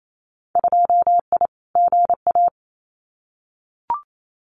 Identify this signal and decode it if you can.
Morse code, heard as a keyed tone, about 28 wpm, 700 Hz. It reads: 2I GA
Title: Winter Field Day Exchanges 28wpm